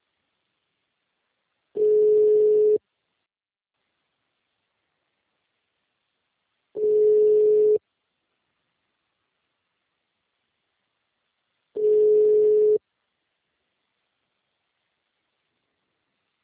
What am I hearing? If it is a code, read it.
Ringback tone (Europe, 425 Hz).
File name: out- -2015- - - .2